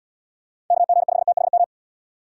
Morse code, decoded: KG5THM

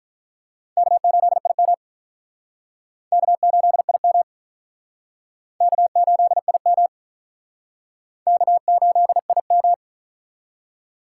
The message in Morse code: K8IM K8IM K8IM K8IM

